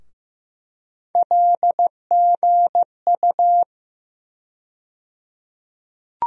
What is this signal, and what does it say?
Morse code, heard as a keyed tone, about 15 wpm, 700 Hz. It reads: LGU
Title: USA Call Sign Postfixes - 3 Letters 15wpm